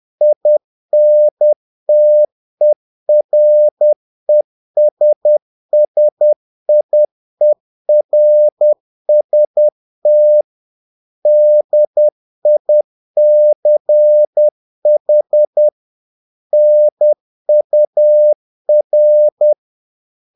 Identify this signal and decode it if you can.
Morse code, heard as a keyed tone, about 10 wpm, 600 Hz. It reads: INTERESSIERST DICH NUR
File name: dk5ke / Humor-50BpM